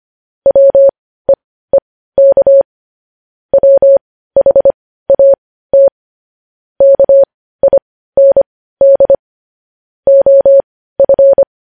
Morse code, heard as a keyed tone, about 25 wpm, spaced as for 15 wpm, 550 Hz. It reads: WEEK WHAT KIND OF
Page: News headlines at 15